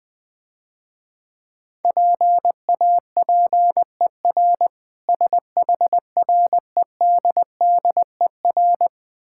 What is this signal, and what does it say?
Morse code, heard as a keyed tone, about 20 wpm, 700 Hz. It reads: PAPER SHREDDER
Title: Easy Words - Speed-Racing with Hints - 20wpm